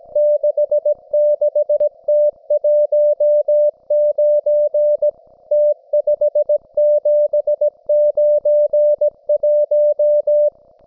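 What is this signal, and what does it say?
Morse code, heard as a keyed tone, about 17 wpm, 590 Hz. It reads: 66T19 T5791
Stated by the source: Russian military Morse station, M32, RDL, on 5268 kHz.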